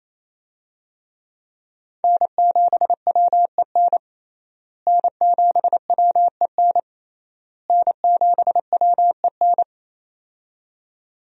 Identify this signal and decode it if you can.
Morse code, heard as a keyed tone, about 27 wpm, 700 Hz. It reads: N7WEN N7WEN N7WEN
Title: USA Call Signs - 5 Characters - Repeated 3x - 28wpm